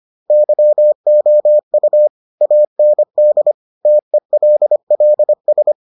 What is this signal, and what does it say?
Morse code, heard as a keyed tone, about 25 wpm, 600 Hz. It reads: YOU AND TELLS